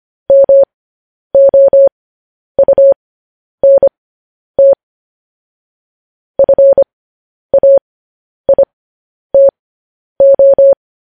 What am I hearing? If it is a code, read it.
Morse code, heard as a keyed tone, about 25 wpm, spaced as for 10 wpm, 550 Hz. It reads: MOUNT FAITO
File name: podcast / News headlines-1x-25x10